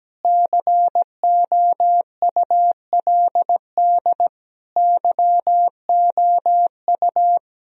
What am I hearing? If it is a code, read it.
Morse code, heard as a keyed tone, about 17 wpm, 700 Hz. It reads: COULD YOU
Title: Sentences from Top 1000 Words - Review Part 2 - 17wpm